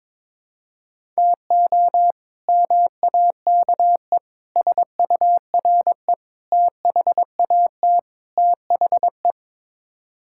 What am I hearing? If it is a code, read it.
Morse code, heard as a keyed tone, about 22 wpm, 700 Hz. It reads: TO MAKE SURE THAT THE
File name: practice / 22wpm / Build A Phrase - 5-Word Phrases - 22wpm